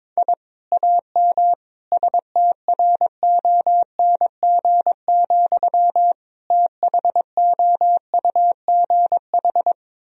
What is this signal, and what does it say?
Morse code, heard as a keyed tone, about 22 wpm, 700 Hz. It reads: I AM STRONG, THOUGH
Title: Sentences from Top 1000 Words - Review Part 3 - 22wpm